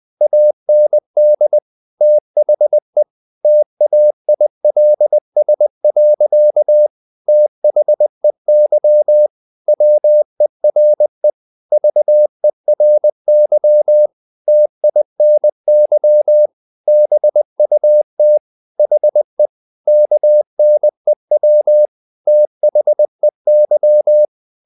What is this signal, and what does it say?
Morse code, heard as a keyed tone, about 20 wpm, 600 Hz. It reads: AND THE TAILS. THEY WERE VERY TINY BUT HE KNEW THEY
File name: the old man and the sea part 2 20wpm